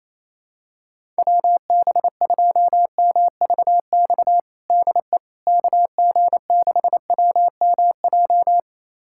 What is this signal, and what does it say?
Morse code, heard as a keyed tone, about 28 wpm, 700 Hz. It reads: WB2MVX DE KG6WMJ
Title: USA Callsign DE Callsign - 6 Characters - 28wpm